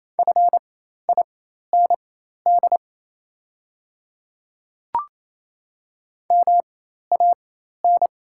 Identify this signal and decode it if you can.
Morse code, heard as a keyed tone, about 27 wpm, spaced as for 13 wpm, 700 Hz. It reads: FIND MAN
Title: 4x Character Spacing - Top 200 Words - 28wpm